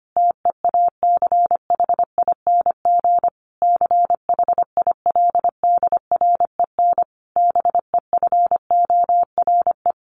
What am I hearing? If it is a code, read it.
Morse code, heard as a keyed tone, about 25 wpm, 700 Hz. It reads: TEACHING CHILDREN BEFORE
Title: Sentences from Top 1100 Words 25wpm